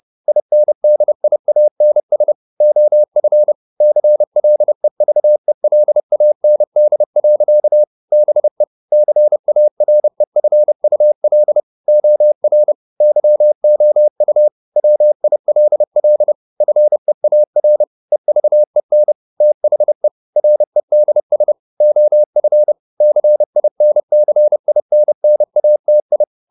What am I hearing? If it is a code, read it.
Morse code, heard as a keyed tone, about 30 wpm, 600 Hz. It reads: INDIANS OF CLEVELAND. BE CAREFUL OR YOU WILL FEAR EVEN THE REDS OF CINCINNATI
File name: the old man and the sea / the old man and the sea 1 30wpm